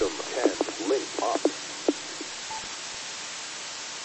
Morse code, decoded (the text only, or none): none